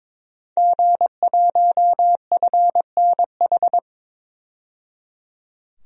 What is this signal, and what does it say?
Morse code, heard as a keyed tone, about 22 wpm, 700 Hz. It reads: G1FNH
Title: DXCC-100 Call Signs - 5 Characters - 22wpm